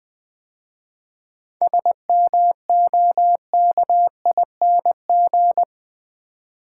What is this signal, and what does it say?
Morse code, heard as a keyed tone, about 20 wpm, 700 Hz. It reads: SMOKING
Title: 7 Letter Words - First 3 Letters Given - 20wpm